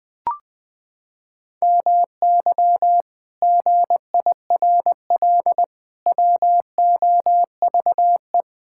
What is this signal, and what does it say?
Morse code, heard as a keyed tone, about 20 wpm, 700 Hz. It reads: MY GIRL WOVE
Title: Pangram Challenge v2 - 20wpm